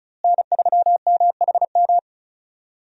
Morse code, heard as a keyed tone, about 36 wpm, 700 Hz. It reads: N3MHM
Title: USA Call Signs - 5 Characters - Repeated 3x - 35wpm